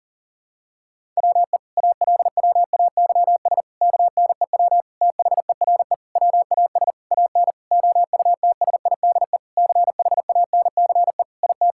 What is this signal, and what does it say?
Morse code, heard as a keyed tone, about 40 wpm, 700 Hz. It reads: WE ALWAYS KNEW THERE WAS AN OUTSIDE CHANCE IT
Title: Sentences from Top 1000 Words - Review Part 6 - 40wpm